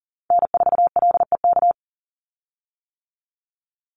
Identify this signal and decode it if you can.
Morse code, heard as a keyed tone, about 40 wpm, 700 Hz. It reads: N4LEK
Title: USA Call Signs 40wpm